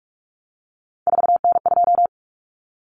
Morse code, heard as a keyed tone, about 44 wpm, 700 Hz. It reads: 4N2
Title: Common Cut Numbers - 3-Digit 45wpm